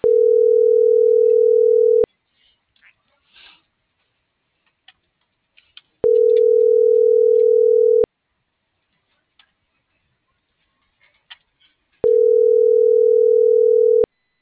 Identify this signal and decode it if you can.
Ringback tone (North America), 440 + 480 Hz.